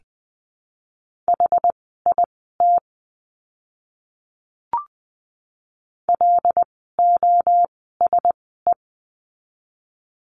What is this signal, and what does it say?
Morse code, heard as a keyed tone, about 20 wpm, spaced as for 14 wpm, 700 Hz. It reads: HIT LOSE
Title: 101 Lesson 17f - Letter u - Words 20wpm